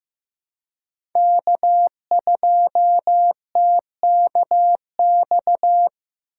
Morse code, heard as a keyed tone, about 15 wpm, 700 Hz. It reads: K2TKX
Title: USA Call Signs - 5 Characters - 4x Character Spacing - 15wpm